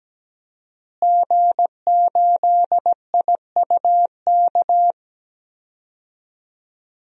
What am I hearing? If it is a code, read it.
Morse code, heard as a keyed tone, about 17 wpm, 700 Hz. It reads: G8IUK
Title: DXCC-100 Call Signs - 5 Characters - Repeated 3x - 17wpm